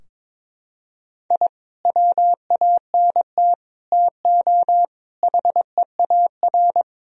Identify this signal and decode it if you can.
Morse code, heard as a keyed tone, about 22 wpm, 700 Hz. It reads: I WANT TO HEAR
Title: Sentences from Top 1000 Words - Review Part 4 - 22wpm